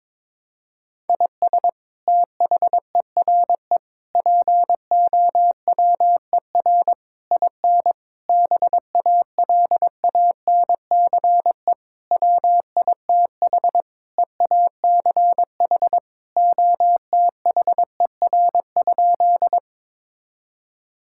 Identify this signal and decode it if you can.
Morse code, heard as a keyed tone, about 22 wpm, 700 Hz. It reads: IS THERE POWER IN BALANCE WITH EACH OTHER?
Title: Sentences from Top 1000 Words - Review Part 6 - 22wpm